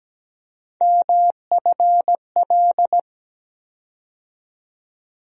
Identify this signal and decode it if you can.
Morse code, heard as a keyed tone, about 17 wpm, 700 Hz. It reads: MFL
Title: USA Call Sign Postfixes - 3 Letters 17wpm